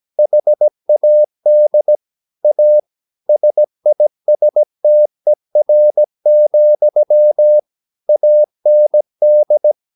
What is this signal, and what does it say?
Morse code, heard as a keyed tone, about 17 wpm, 600 Hz. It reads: HAD A SISTER, AND
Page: Romanen Never af Ken Follet på engelsk.